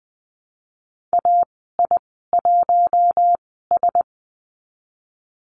Morse code, heard as a keyed tone, about 20 wpm, spaced as for 15 wpm, 700 Hz. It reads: AI1S
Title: 062 Lesson 11g - Letter h - Call Signs 20wpm